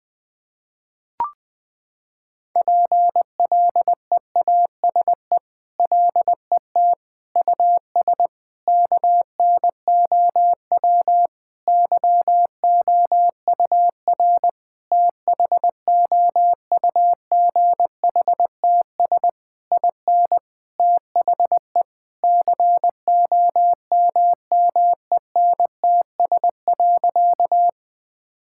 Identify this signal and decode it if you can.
Morse code, heard as a keyed tone, about 19 wpm, 700 Hz. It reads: PLEASE LET US KNOW YOUR THOUGHTS IN THE COMMENTS.
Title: Sentences with 8-Word Phrases - 20wpm